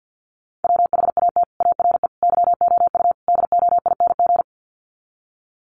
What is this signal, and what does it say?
Morse code, heard as a keyed tone, about 50 wpm, 700 Hz. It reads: WHAT ARE YOU DOING
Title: Sets of 4 Words 50wpm